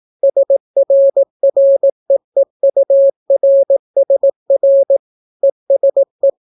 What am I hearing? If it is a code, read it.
Morse code, heard as a keyed tone, about 18 wpm, 550 Hz. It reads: SRREEURSR ESE